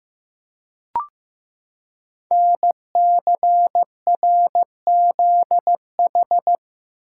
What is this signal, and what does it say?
Morse code, heard as a keyed tone, about 15 wpm, 700 Hz. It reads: NCRZH
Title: Code Groups - 5 Letters - 15wpm